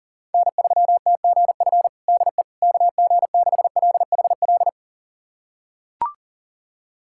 Morse code, heard as a keyed tone, about 40 wpm, 700 Hz. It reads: N3TGF DE KG6LHL
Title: USA Callsign DE Callsign 40wpm